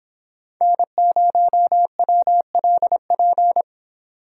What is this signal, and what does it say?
Morse code, heard as a keyed tone, about 25 wpm, 700 Hz. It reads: N0WLP